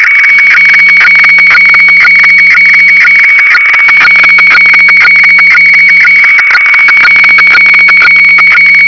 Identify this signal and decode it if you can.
Weather fax (radiofax).